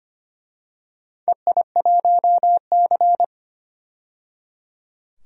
Morse code, heard as a keyed tone, about 25 wpm, 700 Hz. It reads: EI1C
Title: DXCC-100 Call Signs - 4 Characters - 25wpm